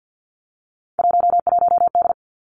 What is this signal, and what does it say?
Morse code, heard as a keyed tone, about 50 wpm, 700 Hz. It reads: 11D
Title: Three Challenging Letter-Number Combinations 50wpm